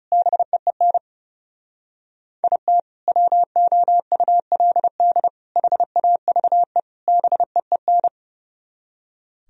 Morse code, heard as a keyed tone, about 36 wpm, 700 Hz. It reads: BEEN IT WOULD HAVE BEEN
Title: Speed-Racing - Sets of 4 Words 30wpm